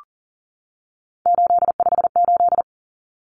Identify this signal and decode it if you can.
Morse code, heard as a keyed tone, about 40 wpm, 700 Hz. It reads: 858
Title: Numbers - 3-Digit 40wpm